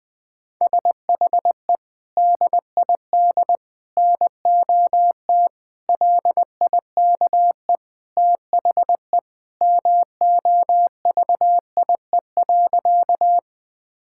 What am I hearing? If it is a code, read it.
Morse code, heard as a keyed tone, about 20 wpm, 700 Hz. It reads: SHE DID NOT LIKE THE MOVIE.
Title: Sentences from Top 1000 Words - Review Part 5 - 20wpm